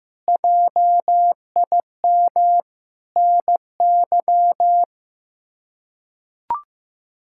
Morse code, heard as a keyed tone, about 16 wpm, 700 Hz. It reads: JIM NY
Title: K1USN SST QSO Practice - 15wpm